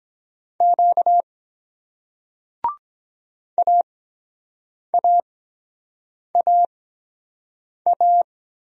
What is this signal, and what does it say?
Morse code, heard as a keyed tone, about 23 wpm, 700 Hz. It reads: Q A A A A